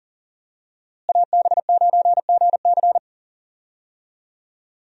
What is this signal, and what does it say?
Morse code, heard as a keyed tone, about 40 wpm, 700 Hz. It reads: AB9GC
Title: USA Call Signs - 5 Characters - Repeated 2x - 40wpm